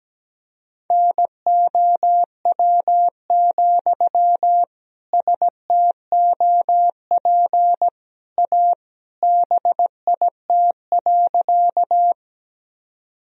Morse code, heard as a keyed tone, about 17 wpm, 700 Hz. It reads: NOW, STOP A BIT.
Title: Sentences from Top 1000 Words - Review Part 1 - 17wpm